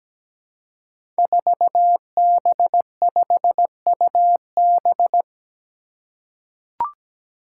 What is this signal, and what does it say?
Morse code, heard as a keyed tone, about 17 wpm, 700 Hz. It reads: 4B5UB
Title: Five Challenging Letter-Number Combinations 17wpm